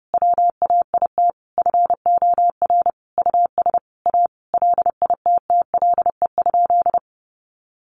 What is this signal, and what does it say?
Morse code, heard as a keyed tone, about 30 wpm, 700 Hz. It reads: WAIT FOR US A LITTLE?
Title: Sentences from Top 900 Words 30wpm